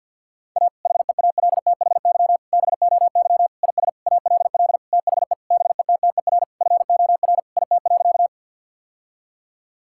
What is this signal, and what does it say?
Morse code, heard as a keyed tone, about 50 wpm, 700 Hz. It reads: A HEALTHY BOY IS ALL THE BETTER FOR IT.
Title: Sentences from Top 1000 Words - Review Part 5 - 50wpm